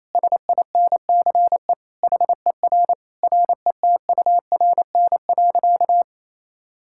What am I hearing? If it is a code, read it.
Morse code, heard as a keyed tone, about 28 wpm, 700 Hz. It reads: SINCE HER RETURN.
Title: Sentences from Top 1000 Words - Review Part 6 - 28wpm